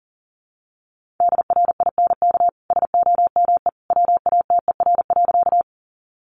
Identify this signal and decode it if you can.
Morse code, heard as a keyed tone, about 40 wpm, 700 Hz. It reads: DRINK SOME WATER.
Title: Sentences from Top 1500 Words 40wpm